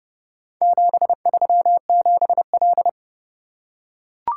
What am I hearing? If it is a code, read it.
Morse code, heard as a keyed tone, about 30 wpm, 700 Hz. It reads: 737L